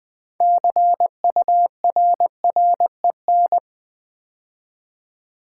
Morse code, heard as a keyed tone, about 20 wpm, 700 Hz. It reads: CURREN
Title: Build A Word - 2 Letters At A Time - 8-Letter Words - 20wpm